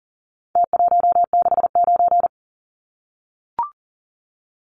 Morse code, heard as a keyed tone, about 40 wpm, 700 Hz. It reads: T169
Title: Numbers - Contest Style - 4-Digit 40wpm